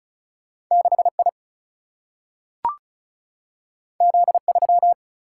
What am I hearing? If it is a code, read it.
Morse code, heard as a keyed tone, about 36 wpm, 700 Hz. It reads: 6 I Z 3
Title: Single-Letter Single-Number 35wpm